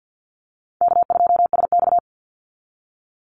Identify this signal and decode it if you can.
Morse code, heard as a keyed tone, about 50 wpm, 700 Hz. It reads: K2SX
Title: Call Signs Repeated 2x - 50wpm